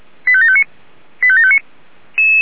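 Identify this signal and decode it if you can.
Five-tone selective call (ZVEI), heard as quick runs of five tones, tones of about 70 ms. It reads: 85759 85759
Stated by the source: ZVEI selective call received by SDR.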